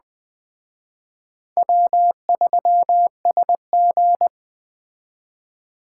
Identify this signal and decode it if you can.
Morse code, heard as a keyed tone, about 20 wpm, 700 Hz. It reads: W3SG
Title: Speed Racing - USA 1x2 Call Signs 17wpm